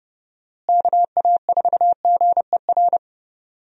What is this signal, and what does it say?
Morse code, heard as a keyed tone, about 32 wpm, 700 Hz. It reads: KA4GER
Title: USA Call Signs - 6 Characters - Repeated 2x - 30wpm